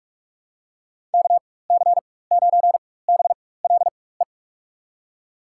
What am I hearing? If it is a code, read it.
Morse code, heard as a keyed tone, about 46 wpm, spaced as for 21 wpm, 700 Hz. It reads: KC9BLE
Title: USA Call Signs - 4x Character Spacing - 45wpm